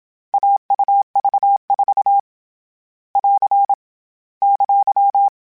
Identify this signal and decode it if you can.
Morse code, heard as a keyed tone, about 27 wpm, 800 Hz. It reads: AUV4 <AR> !